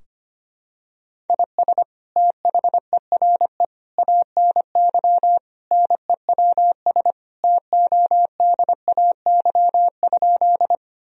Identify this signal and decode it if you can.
Morse code, heard as a keyed tone, about 25 wpm, 700 Hz. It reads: IS THERE ANY NEWS TODAY?
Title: Sentences from Top 1000 Words - Review Part 4 - 25wpm